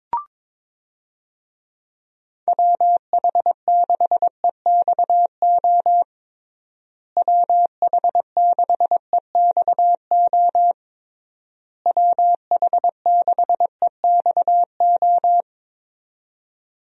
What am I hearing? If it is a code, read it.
Morse code, heard as a keyed tone, about 22 wpm, 700 Hz. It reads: WH6EXO WH6EXO WH6EXO
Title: USA Call Signs - 6 Characters - Repeated 3x - 22wpm